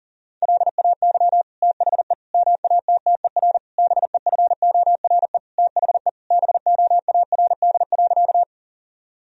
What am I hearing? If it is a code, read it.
Morse code, heard as a keyed tone, about 40 wpm, 700 Hz. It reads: LAY THE MATTER BEFORE THE BOARD.
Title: Sentences from Top 1000 Words - Review Part 4 - 40wpm